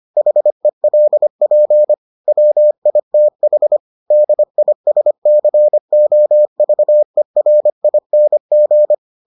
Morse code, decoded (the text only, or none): HELP WITH DISCOVERING